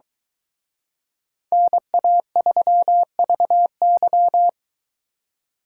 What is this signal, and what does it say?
Morse code, heard as a keyed tone, about 23 wpm, 700 Hz. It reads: NA3VY